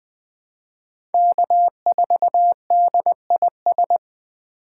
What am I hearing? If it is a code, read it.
Morse code, heard as a keyed tone, about 21 wpm, 700 Hz. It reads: K4DIS
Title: USA Call Signs - 5 Characters - 3x Character Spacing - 20wpm